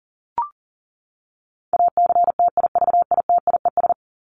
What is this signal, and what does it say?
Morse code, heard as a keyed tone, about 40 wpm, 700 Hz. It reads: ACTIVITIES
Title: ES:S Suffix - Top 100 Words with Base Words 40wpm